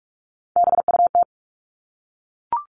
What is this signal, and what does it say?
Morse code, heard as a keyed tone, about 44 wpm, 700 Hz. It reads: BUT
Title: Top 300 Words 45wpm